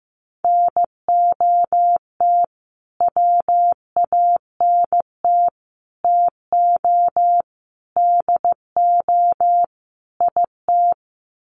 Morse code, heard as a keyed tone, about 15 wpm, 700 Hz. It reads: NOT WANT TO DO IT
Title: Sets of 5 Words Encore 15wpm